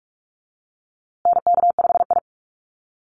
Morse code, heard as a keyed tone, about 44 wpm, 700 Hz. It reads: NK5I
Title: USA Call Signs Repeated 2x 45wpm